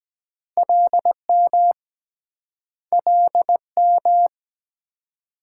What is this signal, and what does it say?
Morse code, heard as a keyed tone, about 22 wpm, 700 Hz. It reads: LM LM